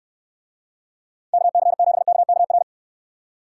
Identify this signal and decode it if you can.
Morse code, heard as a keyed tone, about 69 wpm, 700 Hz. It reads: KC7KKR